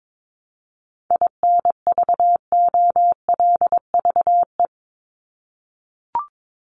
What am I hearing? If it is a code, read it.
Morse code, heard as a keyed tone, about 22 wpm, 700 Hz. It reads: INVOLVE
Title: Top 1001-1250 Words 22wpm